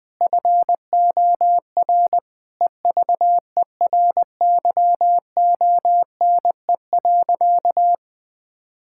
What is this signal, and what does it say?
Morse code, heard as a keyed tone, about 21 wpm, 700 Hz. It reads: FOR EVERYONE.
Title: Sentences from Top 1000 Words - Review Part 5 - 20wpm